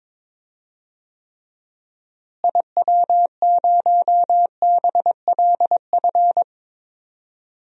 Morse code, heard as a keyed tone, about 22 wpm, 700 Hz. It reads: IW0BLF
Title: DXCC-100 Call Signs - 6 Characters - Repeated 3x - 22wpm